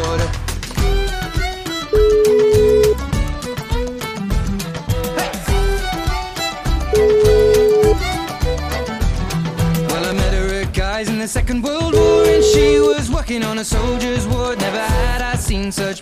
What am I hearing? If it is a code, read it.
Ringback tone (Europe, 425 Hz), with speech.